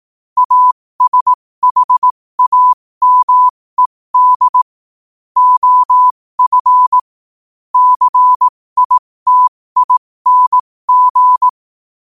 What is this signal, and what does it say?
Morse code, heard as a keyed tone, about 18 wpm, spaced as for 15 wpm, 1000 Hz. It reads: ASHAMED OF CITING
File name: qotdfiles / 15wpm